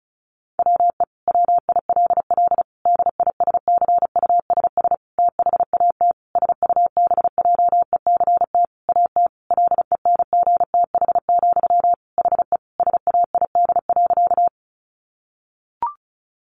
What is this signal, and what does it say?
Morse code, heard as a keyed tone, about 34 wpm, 700 Hz. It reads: WE WILL DISCUSS THAT SUBJECT AT LENGTH, HE SAID.
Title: Sentences from Top 1000 Words 35wpm